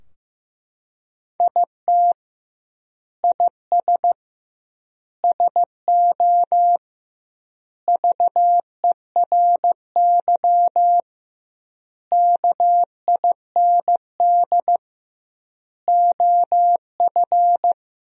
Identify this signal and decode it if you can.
Morse code, heard as a keyed tone, about 15 wpm, 700 Hz. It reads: IT IS SO VERY KIND OF
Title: 2x Word-Spacing - Sentences from Top 300 Words - 15wpm